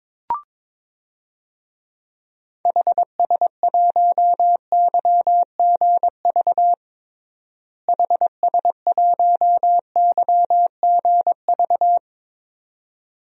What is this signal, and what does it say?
Morse code, heard as a keyed tone, about 22 wpm, 700 Hz. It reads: HS1YGV HS1YGV
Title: DXCC-100 Call Signs - 6 Characters - Repeated 2x - 22wpm